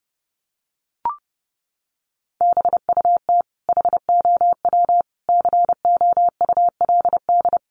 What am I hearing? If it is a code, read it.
Morse code, heard as a keyed tone, about 30 wpm, 700 Hz. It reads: BUT HOW COULD
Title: Sentences from Top 600 Words 30wpm